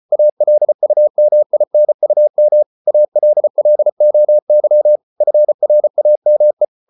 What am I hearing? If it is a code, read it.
Morse code, heard as a keyed tone, about 34 wpm, 600 Hz. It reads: ALUMINUM ALLOY FRAME